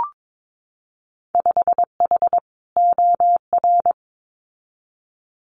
Morse code, heard as a keyed tone, about 22 wpm, 700 Hz. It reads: 5H OR
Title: Winter Field Day Exchanges Encore 22wpm